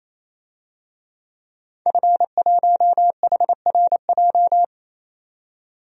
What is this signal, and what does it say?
Morse code, heard as a keyed tone, about 28 wpm, 700 Hz. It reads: F1HRJ
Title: DXCC-100 Call Signs - 5 Characters - 28wpm